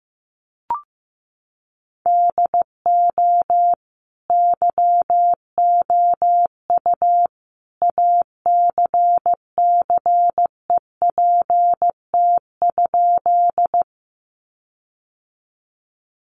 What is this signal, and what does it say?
Morse code, heard as a keyed tone, about 15 wpm, 700 Hz. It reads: DO YOU ACCEPT?
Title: Sentences from Top 1300 Words 15wpm